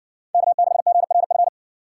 Morse code, heard as a keyed tone, about 60 wpm, 700 Hz. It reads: K6GRF